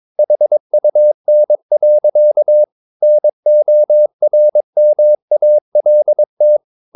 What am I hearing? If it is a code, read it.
Morse code, heard as a keyed tone, about 22 wpm, 600 Hz. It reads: HUN. NORMALT
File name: Ferie QRS 21wpm